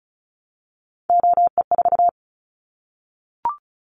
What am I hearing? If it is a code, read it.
Morse code, heard as a keyed tone, about 34 wpm, 700 Hz. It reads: OE4
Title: Callsign Prefixes for Top 100 DXCC - 35wpm